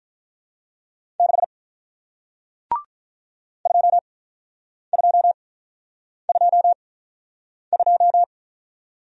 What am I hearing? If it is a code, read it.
Morse code, heard as a keyed tone, about 45 wpm, spaced as for 9 wpm, 700 Hz. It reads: 6 2222